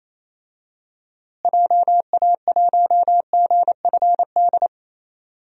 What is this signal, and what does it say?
Morse code, heard as a keyed tone, about 28 wpm, 700 Hz. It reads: JA1GFD
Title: DXCC-100 Call Signs - 6 Characters - Repeated 3x - 28wpm